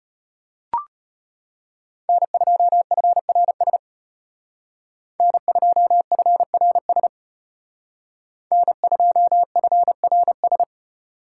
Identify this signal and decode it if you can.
Morse code, heard as a keyed tone, about 32 wpm, 700 Hz. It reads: N2FRS N2FRS N2FRS